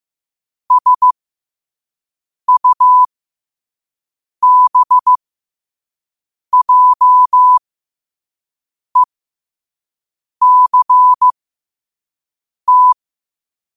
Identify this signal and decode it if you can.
Morse code, heard as a keyed tone, about 15 wpm, spaced as for 5 wpm, 1000 Hz. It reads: SUBJECT
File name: qotdfiles / 5wpm